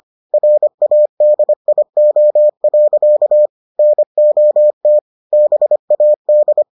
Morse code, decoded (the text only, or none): RADIO. NOT BAD